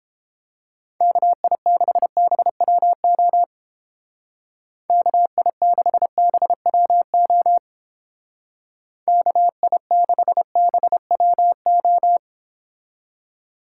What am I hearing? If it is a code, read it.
Morse code, heard as a keyed tone, about 29 wpm, 700 Hz. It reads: KI6BWO KI6BWO KI6BWO